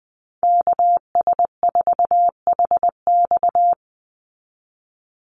Morse code, heard as a keyed tone, about 20 wpm, 700 Hz. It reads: KS4HX
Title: USA Call Signs 20wpm